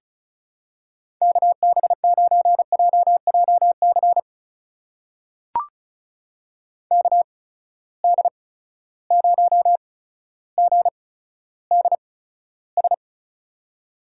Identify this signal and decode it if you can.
Morse code, heard as a keyed tone, about 36 wpm, spaced as for 15 wpm, 700 Hz. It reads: KB9JJC KD0GDS
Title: USA Call Signs - 8x Character Spacing - 35wpm